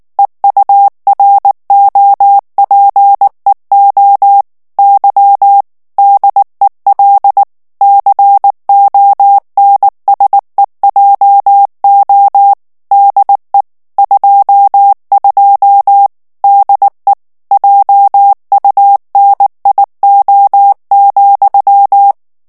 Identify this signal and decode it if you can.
Morse code, heard as a keyed tone, about 19 wpm, 800 Hz. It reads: EUROPEO Y DEL CONSEJO DE 22 DE JUNIO,
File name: morse mp3 / ESPAGNOL TEXTE EN CLAIR 2 VITESSE 18 MOTS MINUTES